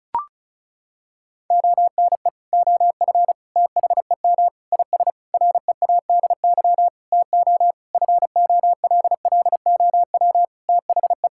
Morse code, heard as a keyed tone, about 36 wpm, 700 Hz. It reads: ONE OF THEM IS READY TO FOLLOW THE
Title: Sentences from Top 1600 Words 35wpm